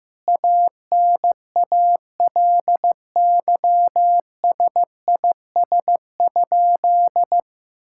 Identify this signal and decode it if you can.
Morse code, heard as a keyed tone, about 15 wpm, 700 Hz. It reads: ANALYSIS?
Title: Sentences from Top 1000 Words - Review Part 5 - 15wpm